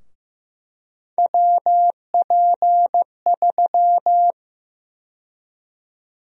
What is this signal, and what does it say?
Morse code, heard as a keyed tone, about 15 wpm, 700 Hz. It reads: WP3
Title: USA Call Sign Prefixes - 15wpm